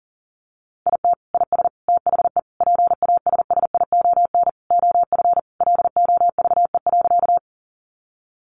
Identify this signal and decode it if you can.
Morse code, heard as a keyed tone, about 40 wpm, 700 Hz. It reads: IT IS THE PASSION OF LOVE.
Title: Sentences from Top 1300 Words 40wpm